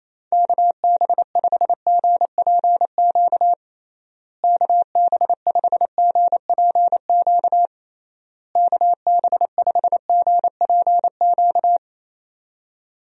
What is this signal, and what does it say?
Morse code, heard as a keyed tone, about 28 wpm, 700 Hz. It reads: KB5GPQ KB5GPQ KB5GPQ
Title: USA Call Signs - 6 Characters - Repeated 3x - 28wpm